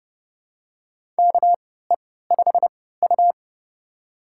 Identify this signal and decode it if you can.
Morse code, heard as a keyed tone, about 30 wpm, spaced as for 17 wpm, 700 Hz. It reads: KE5U
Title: USA Call Signs - 4 Characters - 3x Character Spacing - 30wpm